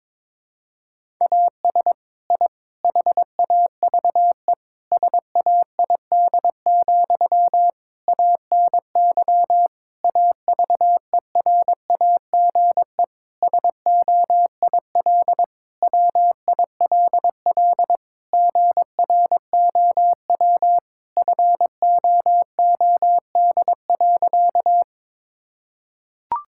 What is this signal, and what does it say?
Morse code, heard as a keyed tone, about 22 wpm, 700 Hz. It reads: AS I HAVE SAID, ANY AVERAGE SOIL WILL GROW FOOD.
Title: Sentences from Top 1000 Words - Review Part 6 - 22wpm